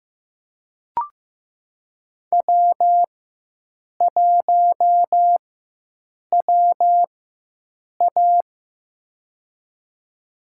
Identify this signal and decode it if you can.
Morse code, heard as a keyed tone, about 15 wpm, spaced as for 7 wpm, 700 Hz. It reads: W1WA
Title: USA Call Signs - 4 Characters - 4x Character Spacing - 15wpm